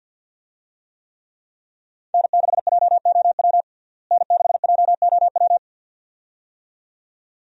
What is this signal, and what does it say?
Morse code, heard as a keyed tone, about 52 wpm, 700 Hz. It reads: N6JOW N6JOW
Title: USA Call Signs - 5 Characters - Repeated 2x - 50wpm